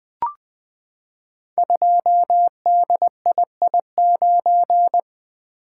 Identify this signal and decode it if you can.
Morse code, heard as a keyed tone, about 20 wpm, 700 Hz. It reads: 2DII9
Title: Code Groups - 5 Letter:Numbers - 20wpm